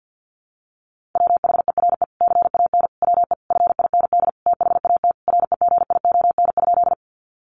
Morse code, heard as a keyed tone, about 50 wpm, 700 Hz. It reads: WHERE CAN WE FIND THAT REGION?
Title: Sentences from Top 1500 Words 50wpm